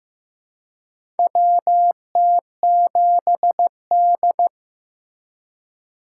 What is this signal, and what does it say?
Morse code, heard as a keyed tone, about 15 wpm, 700 Hz. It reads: WT7D
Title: USA Call Signs - 4 Characters - 4x Character Spacing - 15wpm